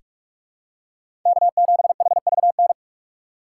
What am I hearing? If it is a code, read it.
Morse code, heard as a keyed tone, about 46 wpm, 700 Hz. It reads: K7HVN
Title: USA Call Signs - 5 Characters - 2x Character Spacing - 45wpm